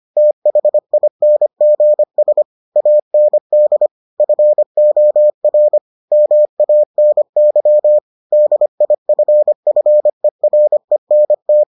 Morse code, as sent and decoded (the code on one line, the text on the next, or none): - .... .. -. --. ... / .- -. -.. / ..-. --- .-. / -- .- -. -.-- / -.. .. ..-. ..-. .-. -. -
THINGS AND FOR MANY DIFFERENT